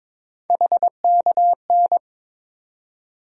Morse code, heard as a keyed tone, about 22 wpm, 700 Hz. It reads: HKN
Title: USA Call Sign Postfixes - 3 Letters 22wpm